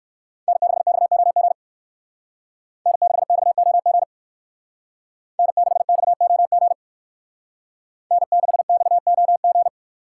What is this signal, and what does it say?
Morse code, heard as a keyed tone, about 59 wpm, 700 Hz. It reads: N6XOG N6XOG N6XOG N6XOG